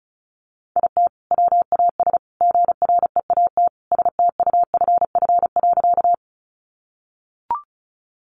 Morse code, heard as a keyed tone, about 34 wpm, 700 Hz. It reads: IT WAS GREAT STUFF.
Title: Sentences from Top 1400 Words 35wpm